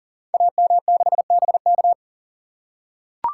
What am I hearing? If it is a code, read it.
Morse code, heard as a keyed tone, about 39 wpm, 700 Hz. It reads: AM6BK
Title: Code Groups - 5 Letter:Numbers - 40wpm